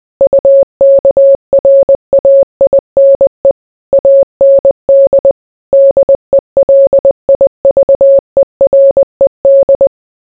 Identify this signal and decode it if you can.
Morse code, heard as a keyed tone, about 20 wpm, 550 Hz. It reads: UKRAINE AND DELIVERED